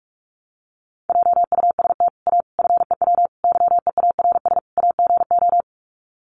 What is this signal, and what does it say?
Morse code, heard as a keyed tone, about 44 wpm, 700 Hz. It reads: JUST A FEW YEARS AGO
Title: Sets of 5 Words Encore 45wpm